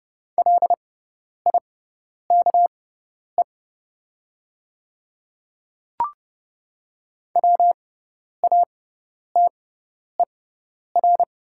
Morse code, heard as a keyed tone, about 30 wpm, spaced as for 10 wpm, 690 Hz. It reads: LIKE WATER
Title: 6x Character Spacing - Top 500 Words - 30wpm